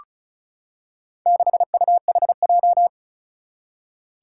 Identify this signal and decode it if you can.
Morse code, heard as a keyed tone, about 36 wpm, 700 Hz. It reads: 6UHJ
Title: Four Challenging Letter-Number Combinations 35wpm